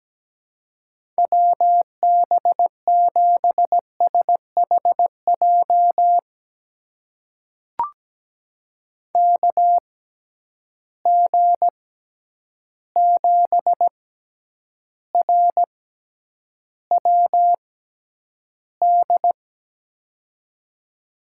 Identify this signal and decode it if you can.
Morse code, heard as a keyed tone, about 17 wpm, spaced as for 9 wpm, 700 Hz. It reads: WB7SHJ KG7RWD
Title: USA Call Signs - 6 Characters - 6x Character Spacing - 17wpm